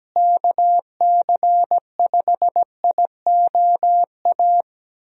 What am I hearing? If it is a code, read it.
Morse code, heard as a keyed tone, about 17 wpm, 700 Hz. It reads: KC5IOA
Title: USA Call Signs - 6 Characters - Repeated 3x - 17wpm